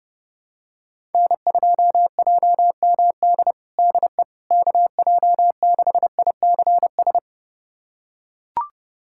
Morse code, heard as a keyed tone, about 30 wpm, 700 Hz. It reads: N2JMD DE KJ6ICS
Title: USA Callsign DE Callsign 30wpm